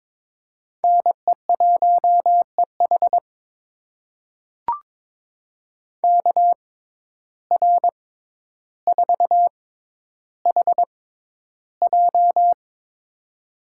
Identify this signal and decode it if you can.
Morse code, heard as a keyed tone, about 22 wpm, spaced as for 11 wpm, 700 Hz. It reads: NE1EH KR4HJ
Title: USA Call Signs - 5 Characters - 6x Character Spacing - 22wpm